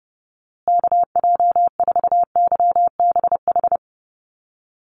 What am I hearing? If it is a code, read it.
Morse code, heard as a keyed tone, about 28 wpm, 700 Hz. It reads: KJ4YBH
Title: USA Call Signs Repeated 2x 30wpm